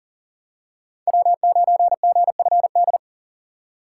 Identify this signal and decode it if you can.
Morse code, heard as a keyed tone, about 40 wpm, 700 Hz. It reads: W9GFD
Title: USA Call Signs - 4x Character Spacing - 40wpm